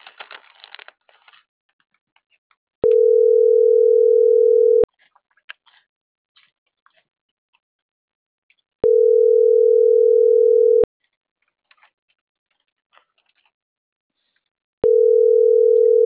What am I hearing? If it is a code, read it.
Ringback tone (North America), 440 + 480 Hz.